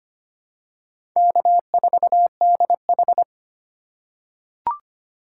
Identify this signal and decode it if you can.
Morse code, heard as a keyed tone, about 25 wpm, 700 Hz. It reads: K4DH
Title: USA Call Signs - 4 Characters - 4x Character Spacing - 25wpm